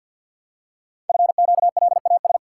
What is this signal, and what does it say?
Morse code, heard as a keyed tone, about 46 wpm, 700 Hz. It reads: FQLAS
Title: Code Groups - 5 Letters - 50wpm